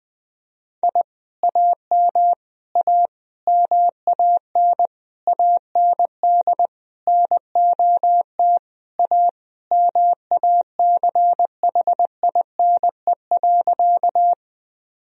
Morse code, heard as a keyed tone, about 20 wpm, 700 Hz. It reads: I AM A MAN AND NOT A MACHINE.
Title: Sentences from Top 1000 Words - Review Part 2 - 20wpm